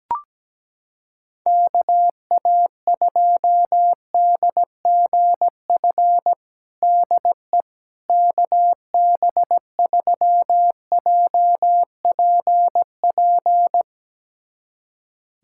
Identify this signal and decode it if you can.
Morse code, heard as a keyed tone, about 17 wpm, 700 Hz. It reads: KA2DGF DE KB3JPP
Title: USA Callsign DE Callsign - 6 Characters - 17wpm